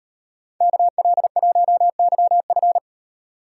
Morse code, heard as a keyed tone, about 38 wpm, 700 Hz. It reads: KL1YF